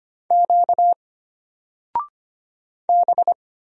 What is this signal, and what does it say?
Morse code, heard as a keyed tone, about 25 wpm, 700 Hz. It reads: Q B
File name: Single-Letter-Number-25wpm